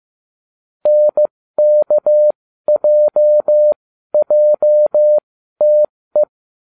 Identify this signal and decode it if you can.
Morse code, heard as a keyed tone, about 15 wpm, spaced as for 12 wpm, 600 Hz. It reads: NKJJTE